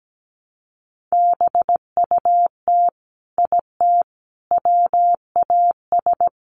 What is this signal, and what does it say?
Morse code, heard as a keyed tone, about 17 wpm, 700 Hz. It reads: BUT IT WAS
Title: Sets of 3 Words 17wpm